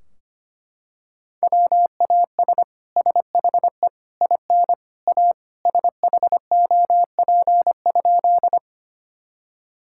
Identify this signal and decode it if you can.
Morse code, heard as a keyed tone, about 25 wpm, 700 Hz. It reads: WAS SHE IN A SHOP?
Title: Sentences from Top 1600 Words 25wpm